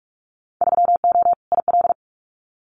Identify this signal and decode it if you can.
Morse code, heard as a keyed tone, about 46 wpm, 700 Hz. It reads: 3O IL
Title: Winter Field Day Exchanges 45wpm